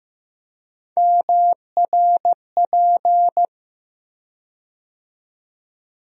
Morse code, heard as a keyed tone, about 15 wpm, 700 Hz. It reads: MRP
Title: USA Call Sign Postfixes - 3 Letters 15wpm